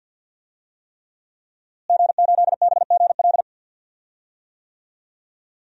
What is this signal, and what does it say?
Morse code, heard as a keyed tone, about 51 wpm, 700 Hz. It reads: G8BGL
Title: DXCC-100 Call Signs - 5 Characters - Repeated 3x - 50wpm